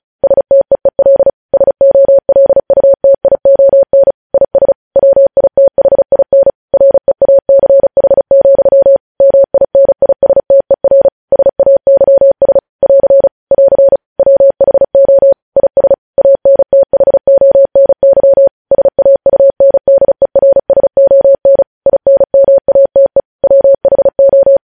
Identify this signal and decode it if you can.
Morse code, heard as a keyed tone, about 36 wpm, 550 Hz. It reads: STEEL SOLUTION IS WITHIN REACH, MINISTER SAYS <AR> <AR> WHO IS ANTHONY SAUNDERSON INMATE WHO
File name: podcast / News headlines-1x-35x35